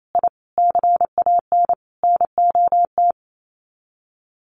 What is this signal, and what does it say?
Morse code, heard as a keyed tone, about 28 wpm, 700 Hz. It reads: I CAN NOT
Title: Sets of 4 Words 28wpm